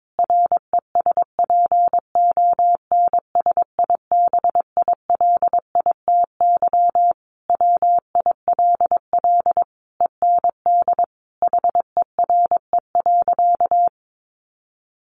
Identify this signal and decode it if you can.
Morse code, heard as a keyed tone, about 22 wpm, 700 Hz. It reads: RESPONSIBILITY WILL END HERE.
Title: Sentences from Top 1200 Words 22wpm